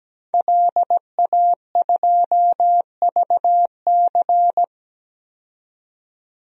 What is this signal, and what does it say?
Morse code, heard as a keyed tone, about 17 wpm, 700 Hz. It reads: LA2VC
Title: DXCC-100 Call Signs - 5 Characters - 17wpm